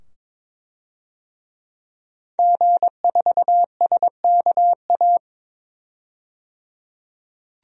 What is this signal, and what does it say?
Morse code, heard as a keyed tone, about 22 wpm, 700 Hz. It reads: G4SKA
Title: DXCC-100 Call Signs - 5 Characters - 22wpm